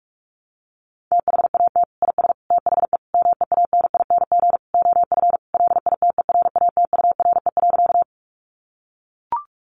Morse code, heard as a keyed tone, about 44 wpm, 700 Hz. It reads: THAT IS THE MEANING OF LITERATURE.
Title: Sentences from Top 900 Words 45wpm